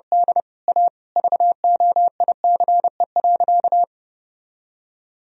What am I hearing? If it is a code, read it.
Morse code, heard as a keyed tone, about 30 wpm, 700 Hz. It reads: D A VOICE.
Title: Sentences from Top 500 Words and Abbreviations 30wpm